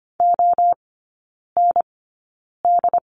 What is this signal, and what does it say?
Morse code, heard as a keyed tone, about 25 wpm, spaced as for 9 wpm, 690 Hz. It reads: OND
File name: Lesson 15e - Number 5 - Three Characters 25wpm